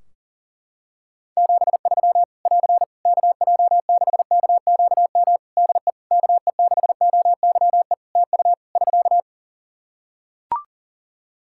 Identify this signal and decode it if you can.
Morse code, heard as a keyed tone, about 40 wpm, 700 Hz. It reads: 73 <AR> KJ6KQM DE KE6OYE TU <SK>
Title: Practice QSO - Exchange 4 - Ending - 40wpm